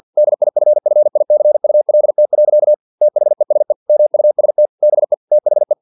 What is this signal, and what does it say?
Morse code, heard as a keyed tone, about 50 wpm, 600 Hz. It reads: DIFFICULT. THESE MUST BE THE
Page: Never 49wpm